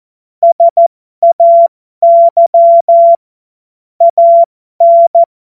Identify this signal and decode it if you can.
Morse code, heard as a keyed tone, about 14 wpm, spaced as for 12 wpm, 680 Hz. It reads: SAY AN
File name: Morsekurs / Witze